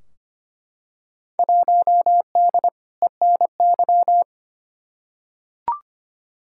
Morse code, heard as a keyed tone, about 25 wpm, 700 Hz. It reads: 1D ENY
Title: Field Day Exchanges - 25wpm